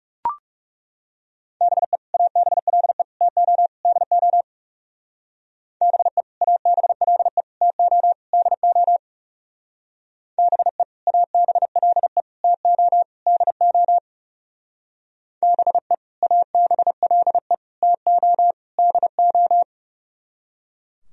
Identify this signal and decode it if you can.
Morse code, heard as a keyed tone, about 34 wpm, 700 Hz. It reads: BE ABLE TO DO BE ABLE TO DO BE ABLE TO DO BE ABLE TO DO
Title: Speed-Racing - Sets of 4 Words 30wpm